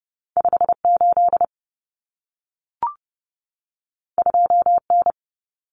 Morse code, heard as a keyed tone, about 30 wpm, 700 Hz. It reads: 58 2N
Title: Common Cut Numbers - 2-Digit 30wpm